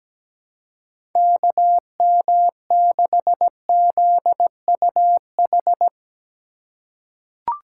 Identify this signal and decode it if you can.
Morse code, heard as a keyed tone, about 17 wpm, 700 Hz. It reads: KM6ZUH
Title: USA Call Signs - 6 Characters - 4x Character Spacing - 17wpm